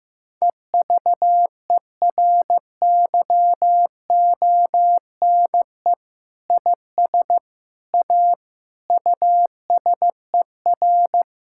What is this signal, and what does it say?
Morse code, heard as a keyed tone, about 15 wpm, 700 Hz. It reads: EVERYONE IS A USER
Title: Sentences from Top 1000 Words - Review Part 5 - 15wpm